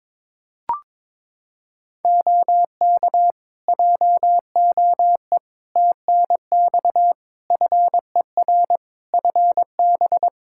Morse code, heard as a keyed tone, about 22 wpm, 700 Hz. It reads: OK JOE TNX FER FB
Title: K1USN SST QSO Practice 22wpm